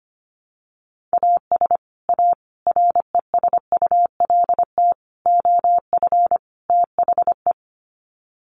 Morse code, heard as a keyed tone, about 25 wpm, 700 Hz. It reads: AS A RESULT OF THE
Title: Sets of 5 Words 25wpm